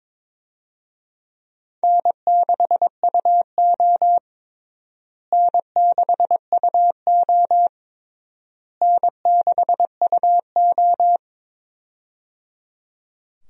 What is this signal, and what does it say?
Morse code, heard as a keyed tone, about 21 wpm, 700 Hz. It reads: N6UO N6UO N6UO
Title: USA Call Signs - 4 Characters - Repeated 3x - 22wpm